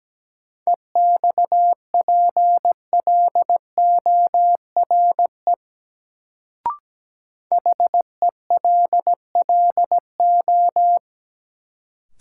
Morse code, heard as a keyed tone, about 17 wpm, 700 Hz. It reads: EXPLORE HELLO
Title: Sentences from Top 1100 Words - New Word Drill - 17wpm